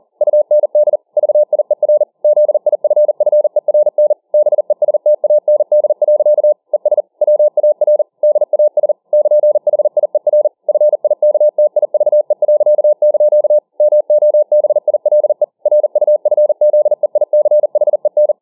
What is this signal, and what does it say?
Morse code, heard as a keyed tone, about 40 wpm, 600 Hz. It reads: UND VIER ZIFFERN BESTAND. ES WAR DAS <KN>HIER FIKTIVE.) MOBILE RUFZEICHEN